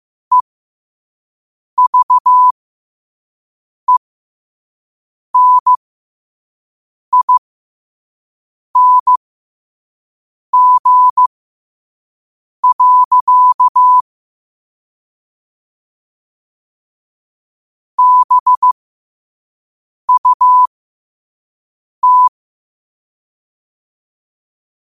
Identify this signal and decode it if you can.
Morse code, heard as a keyed tone, about 15 wpm, spaced as for 5 wpm, 1000 Hz. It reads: EVENING. BUT
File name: qotdfiles / 5wpm